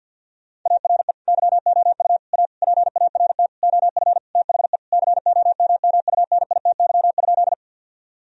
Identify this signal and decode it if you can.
Morse code, heard as a keyed tone, about 50 wpm, 700 Hz. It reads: ARE YOU A PART OF THE COMMUNITY?
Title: Sentences from Top 1000 Words - Review Part 4 - 50wpm